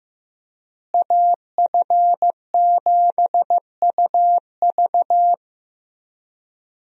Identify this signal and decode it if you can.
Morse code, heard as a keyed tone, about 16 wpm, 700 Hz. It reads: AF7UV
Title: USA Call Signs - 5 Characters - Repeated 2x - 15wpm